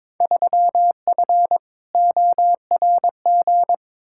Morse code, heard as a keyed tone, about 22 wpm, 700 Hz. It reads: 3F ORG
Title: Field Day Exchanges - 22wpm